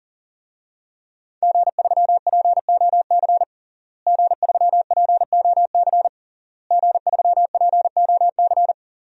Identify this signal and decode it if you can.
Morse code, heard as a keyed tone, about 40 wpm, 700 Hz. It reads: G3POC G3POC G3POC
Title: DXCC-100 Call Signs - 5 Characters - Repeated 3x - 40wpm